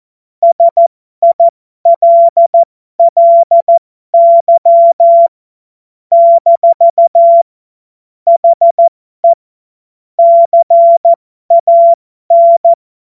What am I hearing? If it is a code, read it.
Morse code, heard as a keyed tone, about 14 wpm, spaced as for 12 wpm, 680 Hz. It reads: SILLY - HE CAN
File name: Morsekurs / Witze